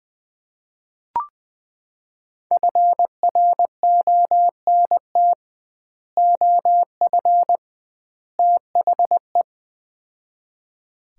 Morse code, heard as a keyed tone, about 21 wpm, 700 Hz. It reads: FRONT OF THE
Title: 2x Word-Spacing - Sets of 3 Words - 20wpm